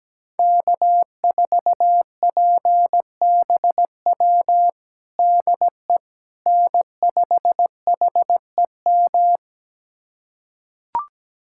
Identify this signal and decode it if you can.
Morse code, heard as a keyed tone, about 17 wpm, 700 Hz. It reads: K4PBW DE N5HEM
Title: USA Callsign DE Callsign - 5 Characters - 17wpm